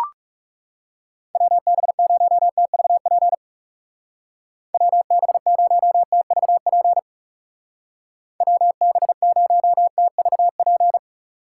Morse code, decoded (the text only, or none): WB0TVP WB0TVP WB0TVP